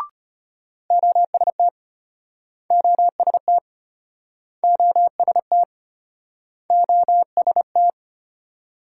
Morse code, heard as a keyed tone, about 35 wpm, 700 Hz. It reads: OST OST OST OST